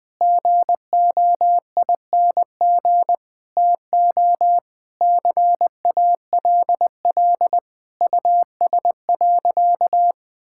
Morse code, heard as a keyed tone, about 20 wpm, 690 Hz. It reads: GOING TO CALL US.
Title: Sentences with 4-Word Phrases - Spoken Beginning - 20wpm